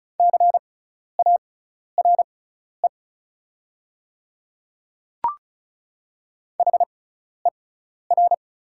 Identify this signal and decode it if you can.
Morse code, heard as a keyed tone, about 36 wpm, spaced as for 12 wpm, 690 Hz. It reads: CARE HER
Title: 6x Character Spacing - Top 300 Words - 35wpm